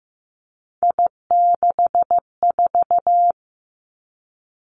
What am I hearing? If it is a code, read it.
Morse code, heard as a keyed tone, about 15 wpm, 700 Hz. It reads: I64
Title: Three Challenging Letter-Number Combinations 15wpm